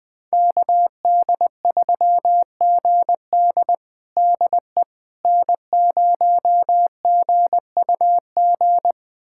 Morse code, heard as a keyed tone, about 20 wpm, 700 Hz. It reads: KD3GD DE N0GUG
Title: USA Callsign DE Callsign - 5 Characters - 20wpm